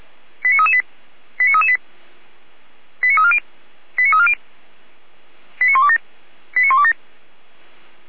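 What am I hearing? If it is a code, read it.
Five-tone selective call (ZVEI), heard as quick runs of five tones, tones of about 70 ms. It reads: 89338 89338 89340 89340 89127 89127
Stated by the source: ZVEI five-tone call received on an SDR, fire brigade Orlamünde.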